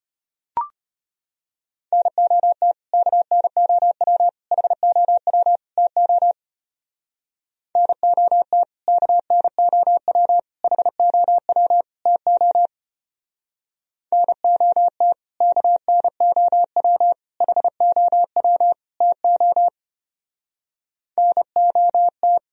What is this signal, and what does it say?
Morse code, heard as a keyed tone, about 32 wpm, 700 Hz. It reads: NOT KNOW HOW TO NOT KNOW HOW TO NOT KNOW HOW TO NOT